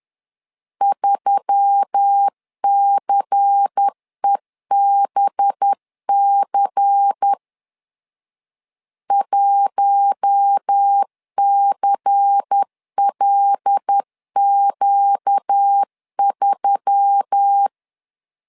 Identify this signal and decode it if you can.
Morse code, heard as a keyed tone, about 10 wpm, 800 Hz. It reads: 3CEBC 1CLQ3
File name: LEZIONI / 5 LBVC / 5 LBVC 10wpm 2349